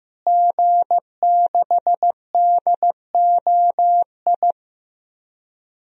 Morse code, heard as a keyed tone, about 15 wpm, 700 Hz. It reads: G6DOI
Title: DXCC-100 Call Signs - 5 Characters - Repeated 3x - 15wpm